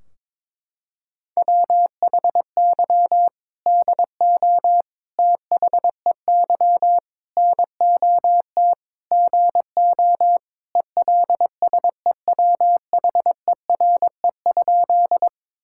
Morse code, heard as a keyed tone, about 22 wpm, 700 Hz. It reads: WHY DO THEY NOT GO ELSEWHERE?
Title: Sentences from Top 1600 Words 22wpm